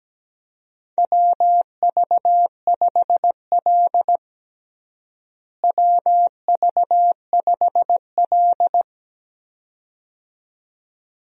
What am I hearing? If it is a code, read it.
Morse code, heard as a keyed tone, about 17 wpm, 700 Hz. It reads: WV5L WV5L
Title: USA Call Signs - 4 Characters - Repeated 2x - 17wpm